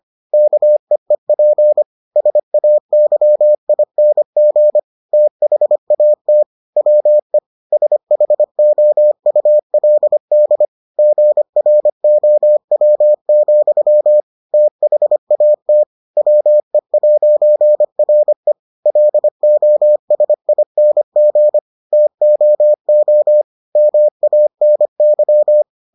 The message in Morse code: KEEP SAYING THAT WE SHOULD GROW, THAT WE'RE LOSING TOO MANY

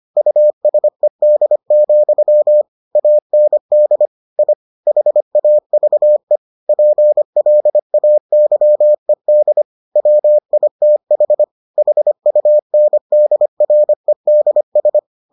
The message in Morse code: USED, AND I HAVE PLAYED WITH HUNDREDS

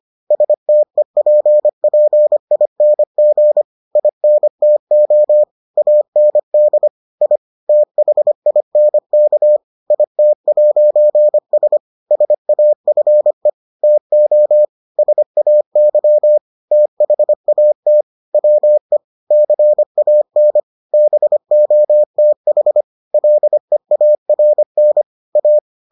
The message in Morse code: STEPPING INTO AND I THINK IT'S SAFE TO SAY THAT WE CAN BOTH LEARN A

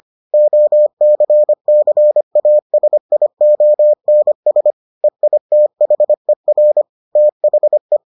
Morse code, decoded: OCCASIONS EITHER THE

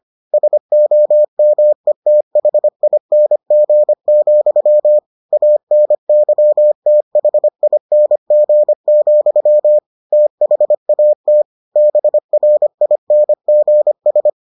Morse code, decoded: SOMETHING, ANYTHING, THAT BRINGS

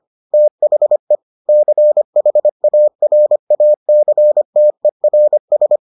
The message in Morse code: THE CHARACTERS

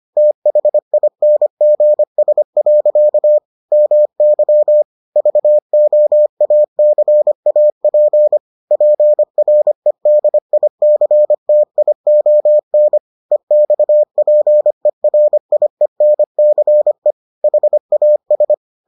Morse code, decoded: THINGS. MY VOACAP PREDICTION EXPERIENCE HAS

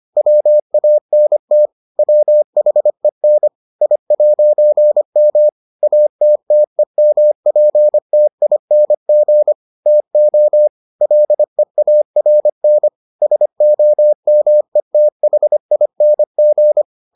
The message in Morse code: WANT WHEN I'M ATTEMPTING TO LEARN SOMETHING

